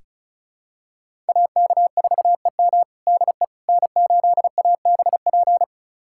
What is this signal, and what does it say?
Morse code, heard as a keyed tone, about 36 wpm, 700 Hz. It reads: AK4EM DE N8ABP
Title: USA Callsign DE Callsign - 5 Characters - 35wpm